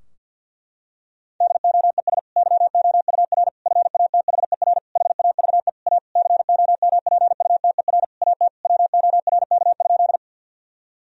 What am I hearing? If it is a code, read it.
Morse code, heard as a keyed tone, about 53 wpm, 700 Hz. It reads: DOES YOUR FATHER HAVE A COMPUTER AT WORK?
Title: Sentences from Top 1000 Words - Review Part 3 - 50wpm